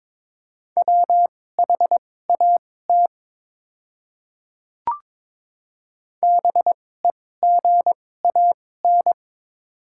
Morse code, heard as a keyed tone, about 22 wpm, spaced as for 16 wpm, 700 Hz. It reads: WHAT BEGAN
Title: 2x Character Spacing - Top 300 Words - 22wpm